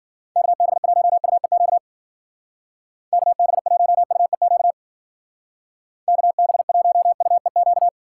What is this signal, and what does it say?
Morse code, heard as a keyed tone, about 48 wpm, 700 Hz. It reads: KB1UEQ KB1UEQ KB1UEQ
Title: Speed Racing - USA 2x3 Call Signs 40wpm